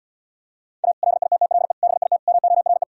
Morse code, heard as a keyed tone, about 77 wpm, 700 Hz. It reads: A LITTLE BIT MORE